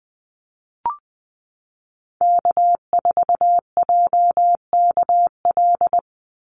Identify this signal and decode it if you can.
Morse code, heard as a keyed tone, about 20 wpm, 700 Hz. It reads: K4JKL
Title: Call Signs Repeated 2x - 20wpm